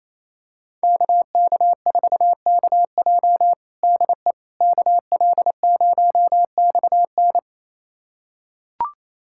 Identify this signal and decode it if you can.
Morse code, heard as a keyed tone, about 28 wpm, 700 Hz. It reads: KK4KJ DE KL0XN
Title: USA Callsign DE Callsign - 5 Characters - 28wpm